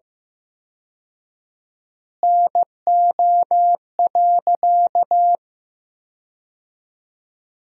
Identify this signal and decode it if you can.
Morse code, heard as a keyed tone, about 15 wpm, 700 Hz. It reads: NO.
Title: 4x Word-Spacing - Sentences from Top 300 Words - 15wpm